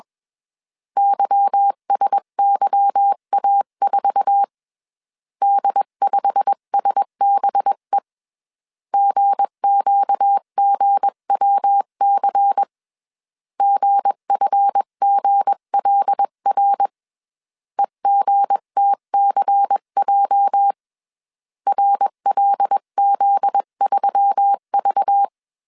Morse code, heard as a keyed tone, about 21 wpm, 800 Hz. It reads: YSYA4 D5SBE GQGWC GFGLR EGTCJ RLZ3V
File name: CW / DET / 1 9 1500C S B 20wpm 2313